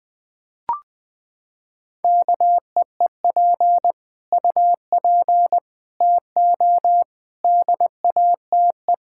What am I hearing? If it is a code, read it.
Morse code, heard as a keyed tone, about 20 wpm, 700 Hz. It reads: KEEP UP TO DATE
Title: Sentences with 9-Word Phrases - 20wpm